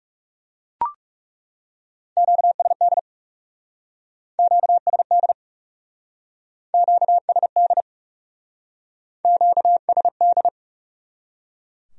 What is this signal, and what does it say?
Morse code, heard as a keyed tone, about 38 wpm, 700 Hz. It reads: QSD QSD QSD QSD